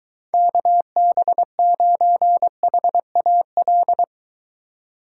Morse code, heard as a keyed tone, about 22 wpm, 700 Hz. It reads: KB9HAL